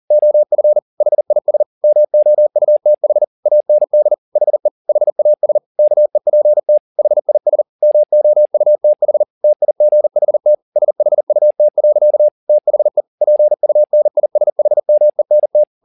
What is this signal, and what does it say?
Morse code, heard as a keyed tone, about 40 wpm, 600 Hz. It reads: OF HIS MOUTH AND HE HAS KEPT HIS MOUTH TIGHT SHUT. THE PUNISHMENT
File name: the old man and the sea 2 40wpm